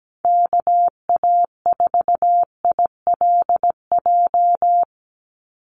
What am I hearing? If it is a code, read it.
Morse code, heard as a keyed tone, about 17 wpm, 700 Hz. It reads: KA4ILJ
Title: USA Call Signs 17wpm